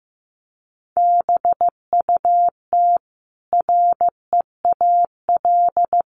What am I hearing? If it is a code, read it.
Morse code, heard as a keyed tone, about 15 wpm, 700 Hz. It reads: BUT REAL
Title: Sentences from Top 1400 Words 15wpm